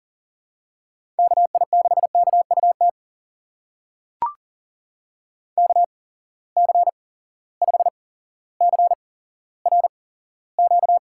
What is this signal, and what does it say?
Morse code, heard as a keyed tone, about 40 wpm, spaced as for 17 wpm, 700 Hz. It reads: KI6KUT KC5CRQ
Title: USA Call Signs - 6 Characters - 8x Character Spacing - 40wpm